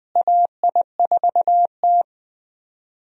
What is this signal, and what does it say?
Morse code, heard as a keyed tone, about 20 wpm, 700 Hz. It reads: AI4T
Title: USA Call Signs - 4 Characters - Repeated 3x - 20wpm